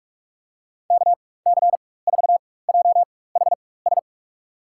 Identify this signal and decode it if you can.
Morse code, heard as a keyed tone, about 46 wpm, spaced as for 21 wpm, 700 Hz. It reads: KC4JHS
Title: USA Call Signs - 4x Character Spacing - 45wpm